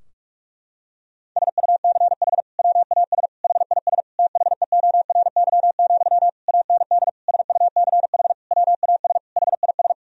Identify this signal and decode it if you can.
Morse code, heard as a keyed tone, about 46 wpm, 700 Hz. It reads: SUCH WAS HIS THEORY, AND SUCH WAS HIS
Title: Sentences from Top 1000 Words - Review Part 4 - 45wpm